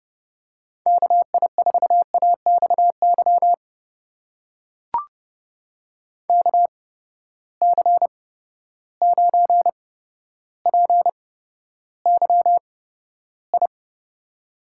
Morse code, heard as a keyed tone, about 30 wpm, spaced as for 13 wpm, 700 Hz. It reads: KI4AXY KC9PYI
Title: USA Call Signs - 6 Characters - 8x Character Spacing - 30wpm